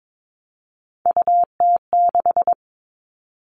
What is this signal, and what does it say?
Morse code, heard as a keyed tone, about 22 wpm, 700 Hz. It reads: UT6
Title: All Cut Numbers - 3-Digit 22wpm